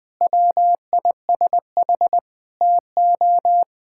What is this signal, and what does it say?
Morse code, heard as a keyed tone, about 20 wpm, 700 Hz. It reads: WISH TO
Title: Sentences from Top 1600 Words 20wpm